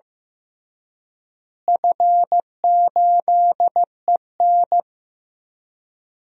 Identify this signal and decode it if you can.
Morse code, heard as a keyed tone, about 15 wpm, 700 Hz. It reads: F8EN
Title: DXCC-100 Call Signs - 4 Characters - Repeated 2x - 15wpm